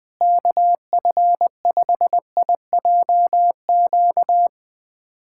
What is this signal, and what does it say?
Morse code, heard as a keyed tone, about 20 wpm, 700 Hz. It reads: KF5IJQ
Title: USA Callsign DE Callsign - 6 Characters - 20wpm